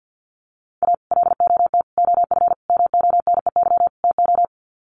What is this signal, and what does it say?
Morse code, heard as a keyed tone, about 50 wpm, 700 Hz. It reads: A LOT OF MONEY TO
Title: Sets of 5 Words Encore2 50wpm